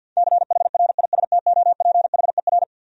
Morse code, heard as a keyed tone, about 52 wpm, 700 Hz. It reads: CHRISTOPHER
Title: USA Common Names - 50wpm